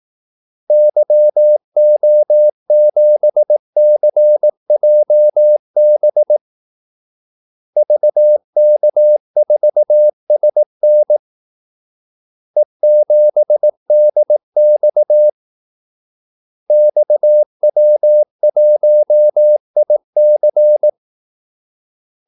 Morse code, heard as a keyed tone, about 18 wpm, 600 Hz. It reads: YO7CJB VK4SN E7DX XW1IC
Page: Kaldesignaler 18 wpm